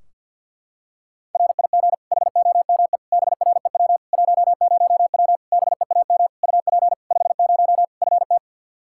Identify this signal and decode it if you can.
Morse code, heard as a keyed tone, about 46 wpm, 700 Hz. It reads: RIG HOME BREW 10W BEAM UP 50 FT
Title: Practice QSO - Exchange 2 - Info - 50wpm